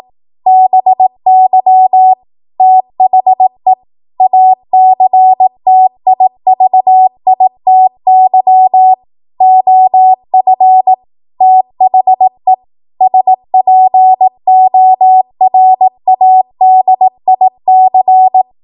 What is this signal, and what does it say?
Morse code, heard as a keyed tone, about 18 wpm, 750 Hz. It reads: BY THE ACTIVITY OF THE SPORADIC
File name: Morse / 18WPM